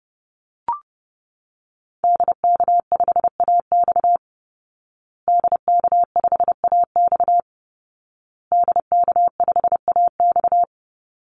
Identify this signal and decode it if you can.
Morse code, heard as a keyed tone, about 30 wpm, 700 Hz. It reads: DK5AX DK5AX DK5AX
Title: Call Signs Repeated 2x - 30wpm